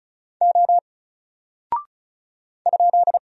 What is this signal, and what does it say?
Morse code, heard as a keyed tone, about 36 wpm, 700 Hz. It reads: O ?
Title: Single Character (Letter:Number:Punctuation:Prosign) 35wpm